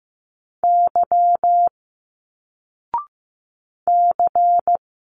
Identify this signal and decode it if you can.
Morse code, heard as a keyed tone, about 15 wpm, 700 Hz. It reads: Y C
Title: Single Letters 15wpm